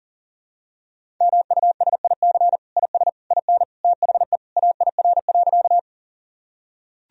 Morse code, heard as a keyed tone, about 40 wpm, 700 Hz. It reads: MUSIC IS IN THE AIR.
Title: Sentences from Top 1000 Words - Review Part 1 - 40wpm